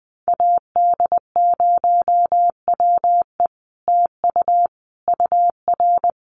Morse code, heard as a keyed tone, about 20 wpm, 700 Hz. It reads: AD0WE TU UR
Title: Common QSO Phrases Encore 20wpm